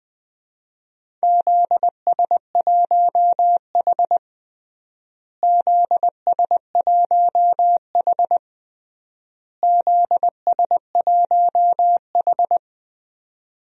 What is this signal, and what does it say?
Morse code, heard as a keyed tone, about 20 wpm, 700 Hz. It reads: ZS1H ZS1H ZS1H
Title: DXCC-100 Call Signs - 4 Characters - Repeated 3x - 20wpm